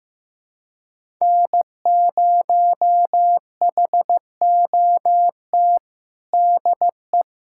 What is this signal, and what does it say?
Morse code, heard as a keyed tone, about 15 wpm, 700 Hz. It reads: N0HOT DE
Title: USA Callsign DE Callsign - 5 Characters - 15wpm